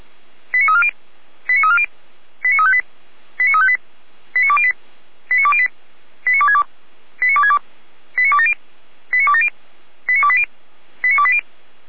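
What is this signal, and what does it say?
Five-tone selective call (ZVEI), heard as quick runs of five tones, tones of about 70 ms. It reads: 89340 89340 89358 89358 89208 89208 89262 89262 89270 89270 89280 89280
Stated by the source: ZVEI five-tone call received by SDR.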